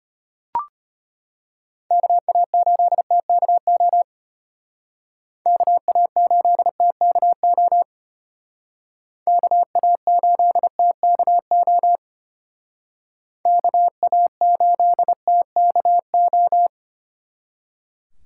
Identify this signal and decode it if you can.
Morse code, heard as a keyed tone, about 32 wpm, 700 Hz. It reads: KA8TKO KA8TKO KA8TKO KA8TKO